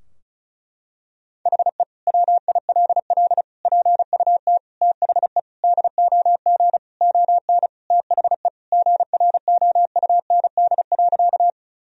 Morse code, heard as a keyed tone, about 34 wpm, 700 Hz. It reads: HE WILL PUT THE DOG ON THE GROUND.
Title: Sentences from Top 1000 Words - Review Part 2 - 35wpm